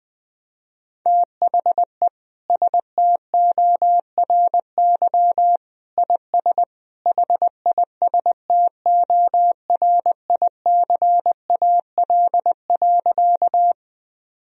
Morse code, heard as a keyed tone, about 21 wpm, 700 Hz. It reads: THE STORY IS HISTORICAL.
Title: Sentences from Top 1000 Words - Review Part 4 - 20wpm